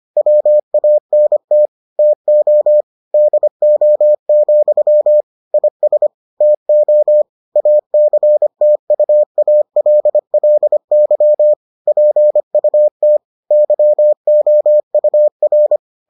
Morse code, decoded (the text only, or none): WANT TO DO, IS TO ACTUALLY PUT YOUR